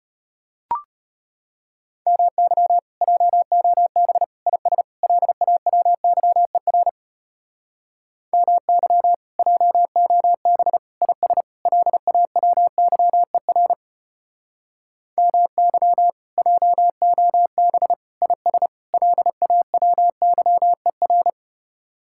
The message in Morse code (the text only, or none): MY JOB IS LAWYER MY JOB IS LAWYER MY JOB IS LAWYER